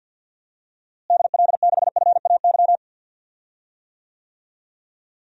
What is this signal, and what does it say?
Morse code, heard as a keyed tone, about 51 wpm, 700 Hz. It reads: DL6FAY
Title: DXCC-100 Call Signs - 6 Characters - Repeated 3x - 50wpm